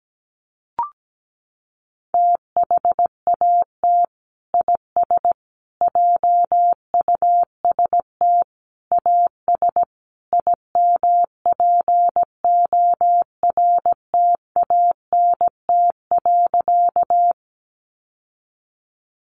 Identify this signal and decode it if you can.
Morse code, heard as a keyed tone, about 17 wpm, 700 Hz. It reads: THAT IS JUST AS IMPORTANT.
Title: Sentences from Top 600 Words 17wpm